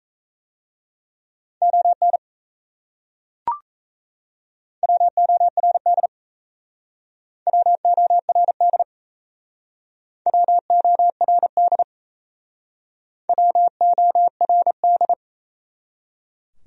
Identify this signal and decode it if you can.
Morse code, heard as a keyed tone, about 38 wpm, 700 Hz. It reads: ON WORD WORD WORD WORD